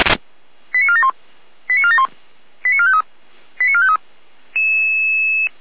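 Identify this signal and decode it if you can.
Five-tone selective call (ZVEI), heard as quick runs of five tones, tones of about 70 ms. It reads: 89471 89471 89453 89453
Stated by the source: Five-tone ZVEI selective call received by SDR, fire brigade (Feuerwehr) Orlamünde.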